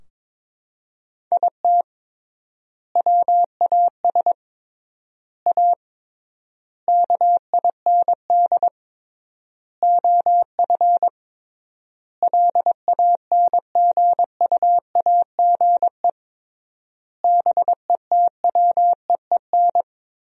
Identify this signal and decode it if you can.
Morse code, heard as a keyed tone, about 22 wpm, 700 Hz. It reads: IT WAS A KIND OF LANGUAGE BETWEEN
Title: 3x Word-Spacing - Sentences from Top 500 Words - 22wpm